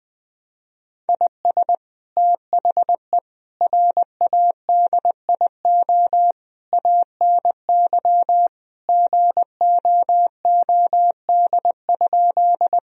Morse code, heard as a keyed tone, about 20 wpm, 700 Hz. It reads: IS THE RADIO ANY GOOD?
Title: Sentences from Top 1000 Words - Review Part 3 - 20wpm